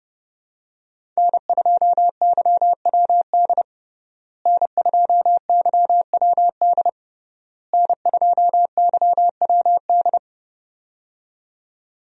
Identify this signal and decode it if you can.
Morse code, heard as a keyed tone, about 30 wpm, 700 Hz. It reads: N2YWD N2YWD N2YWD
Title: USA Call Signs - 5 Characters - Repeated 3x - 30wpm